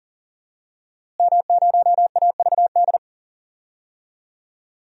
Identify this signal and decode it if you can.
Morse code, heard as a keyed tone, about 40 wpm, 700 Hz. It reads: M0AVD
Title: DXCC-100 Call Signs - 5 Characters - Repeated 2x - 40wpm